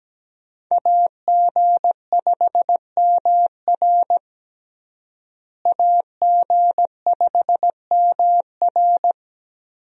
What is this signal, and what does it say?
Morse code, heard as a keyed tone, about 17 wpm, 700 Hz. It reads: AG5MR AG5MR
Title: USA Call Signs - 5 Characters - Repeated 3x - 17wpm